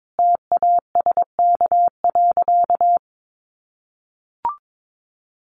Morse code, heard as a keyed tone, about 22 wpm, 700 Hz. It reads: TASK.
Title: Sentences from Top 1400 Words 22wpm